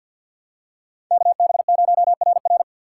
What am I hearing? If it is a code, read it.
Morse code, heard as a keyed tone, about 52 wpm, 690 Hz. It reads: KB0RR
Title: USA Call Signs - 2x Character Spacing - 50wpm